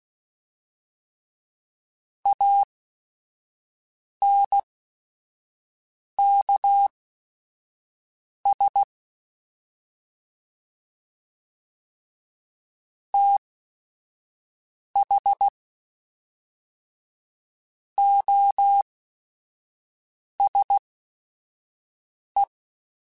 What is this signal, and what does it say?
Morse code, heard as a keyed tone, about 16 wpm, spaced as for 5 wpm, 800 Hz. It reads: ANKS TH OSE